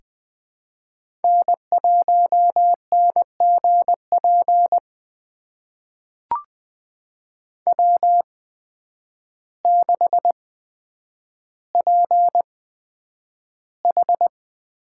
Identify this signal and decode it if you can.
Morse code, heard as a keyed tone, about 20 wpm, spaced as for 9 wpm, 700 Hz. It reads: N1NGP W6PH
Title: USA Call Signs - 8x Character Spacing - 20wpm